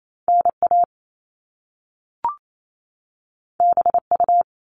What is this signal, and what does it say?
Morse code, heard as a keyed tone, about 28 wpm, 700 Hz. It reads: NA BU
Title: Top Two Letter Combinations in Words 28wpm